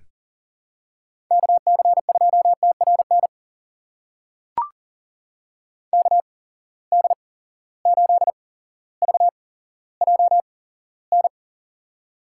Morse code, heard as a keyed tone, about 43 wpm, spaced as for 17 wpm, 700 Hz. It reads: KC2TRN KD8VJN
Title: USA Call Signs - 6 Characters - 8x Character Spacing - 40wpm